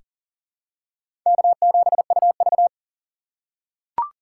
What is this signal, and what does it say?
Morse code, heard as a keyed tone, about 40 wpm, 700 Hz. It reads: K7UV
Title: USA Call Signs - 4 Characters - 6x Character Spacing - 40wpm